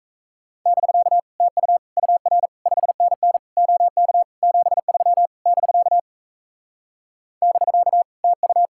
Morse code, heard as a keyed tone, about 37 wpm, 700 Hz. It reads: <BK> TU UR 5NN OK 73 <BK> <BK> TU